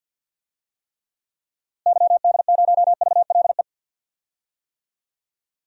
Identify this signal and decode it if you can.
Morse code, heard as a keyed tone, about 52 wpm, 700 Hz. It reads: YD0VLE
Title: DXCC-100 Call Signs - 6 Characters - Repeated 3x - 50wpm